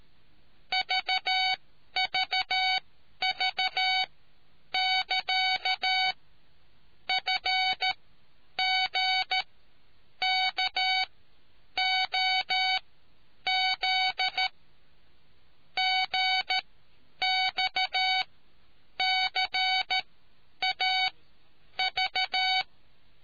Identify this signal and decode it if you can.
Morse code, heard as a keyed tone, about 13 wpm, spaced as for 8 wpm, 750 Hz. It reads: VVV<KA>FGKOZ GXCAV